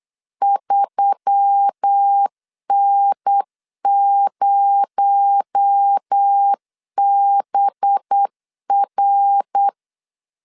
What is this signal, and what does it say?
Morse code, heard as a keyed tone, about 8 wpm, 800 Hz. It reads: 3N0BR
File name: LEZIONI / 7 GF9Z / 7 GF9Z 8wpm 3213